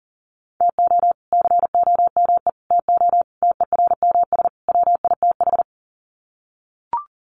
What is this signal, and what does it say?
Morse code, heard as a keyed tone, about 40 wpm, 700 Hz. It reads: TO COME TO TERMS WITH
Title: Sets of 5 Words Encore 40wpm